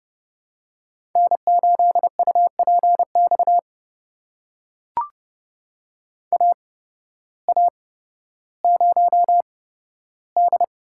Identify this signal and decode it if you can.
Morse code, heard as a keyed tone, about 32 wpm, spaced as for 14 wpm, 700 Hz. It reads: N8UPX AA0D
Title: USA Call Signs - 5 Characters - 8x Character Spacing - 30wpm